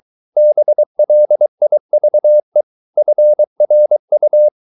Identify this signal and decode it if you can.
Morse code, heard as a keyed tone, about 23 wpm, 600 Hz. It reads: BLIVE FRU
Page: Ferie 23 wpm